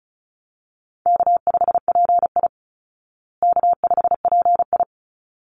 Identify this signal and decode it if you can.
Morse code, heard as a keyed tone, about 34 wpm, 700 Hz. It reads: K5PI K5PI
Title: Easy Call Signs Repeated 2x - 35wpm